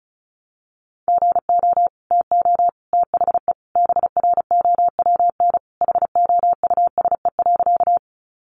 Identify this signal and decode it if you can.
Morse code, heard as a keyed tone, about 34 wpm, 700 Hz. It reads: GO TO THE BROWN HOUSE.
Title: Sentences from Top 1200 Words 35wpm